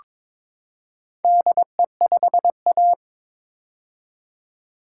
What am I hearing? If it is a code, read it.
Morse code, heard as a keyed tone, about 21 wpm, 700 Hz. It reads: DE5A
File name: practice / 22wpm / Four Letter-Number Combinations - 22wpm